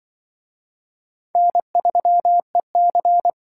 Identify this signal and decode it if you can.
Morse code, heard as a keyed tone, about 25 wpm, 700 Hz. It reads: N3EC
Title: Speed Racing - USA 1x2 Call Signs 20wpm